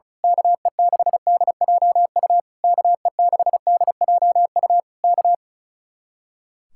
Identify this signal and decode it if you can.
Morse code, heard as a keyed tone, about 33 wpm, 700 Hz. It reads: KE6DJU KE6DJU K
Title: Speed-Racing Common QSO Phrases - 35wpm